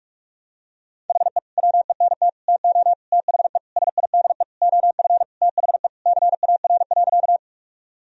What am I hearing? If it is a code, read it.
Morse code, heard as a keyed tone, about 41 wpm, 700 Hz. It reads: HE WENT TO THE SIDE OF THE CAR.
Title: Sentences from Top 1000 Words - Review Part 1 - 45wpm